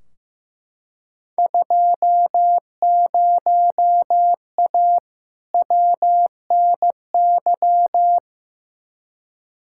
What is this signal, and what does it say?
Morse code, heard as a keyed tone, about 15 wpm, 700 Hz. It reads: 20A WNY
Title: Field Day Exchanges Encore - 15wpm